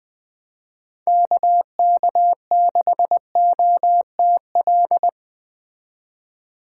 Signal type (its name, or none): Morse code